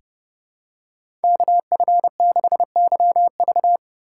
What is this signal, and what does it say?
Morse code, heard as a keyed tone, about 31 wpm, 700 Hz. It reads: KF6YV
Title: USA Call Signs - 5 Characters - 4x Character Spacing - 30wpm